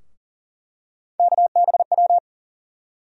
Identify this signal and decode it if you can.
Morse code, heard as a keyed tone, about 40 wpm, 700 Hz. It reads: KBW
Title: USA Call Sign Postfixes - 3 Letters 40wpm